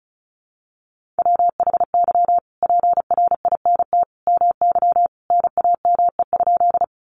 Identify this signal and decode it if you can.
Morse code, heard as a keyed tone, about 34 wpm, 700 Hz. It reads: WHY PRINT MY NAME?
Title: Sentences from Top 1400 Words 35wpm